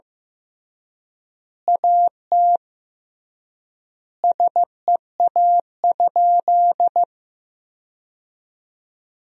Morse code, heard as a keyed tone, about 15 wpm, 700 Hz. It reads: AT SEA?
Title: 3x Word-Spacing - Sentences from Top 300 Words - 15wpm